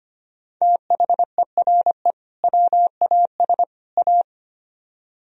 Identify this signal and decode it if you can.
Morse code, heard as a keyed tone, about 32 wpm, 700 Hz. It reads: THERE WAS A